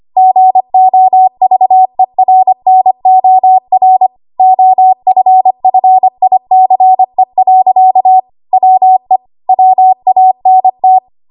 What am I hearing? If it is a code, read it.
Morse code, heard as a keyed tone, about 25 wpm, 750 Hz. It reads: GOVERNOR OFFICE. WE WANT
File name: Morse / 25WPM